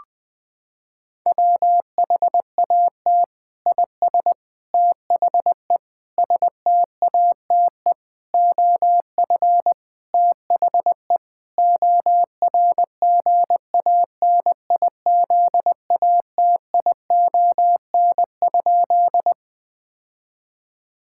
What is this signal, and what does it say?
Morse code, heard as a keyed tone, about 20 wpm, 700 Hz. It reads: WHAT IS THE STATE OF THE ORGANIZATION?
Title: Sentences from Top 1000 Words - Review Part 5 - 20wpm